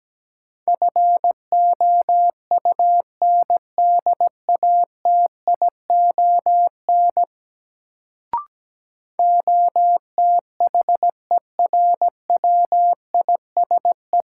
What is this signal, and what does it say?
Morse code, heard as a keyed tone, about 17 wpm, 700 Hz. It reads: FOUNDATION OTHERWISE
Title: Sentences from Top 1200 Words - New Word Drill Alt - 17wpm